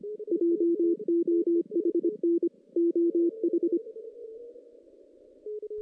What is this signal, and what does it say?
Morse code, heard as a keyed tone, about 25 wpm, 340 Hz. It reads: JOHN OH
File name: 2016 JAN CW NAQP / 2016 Jan naqp k5ot - 0-5